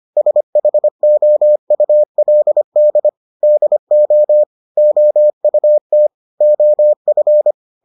Morse code, heard as a keyed tone, about 25 wpm, 600 Hz. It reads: SHOULD DO OUT OF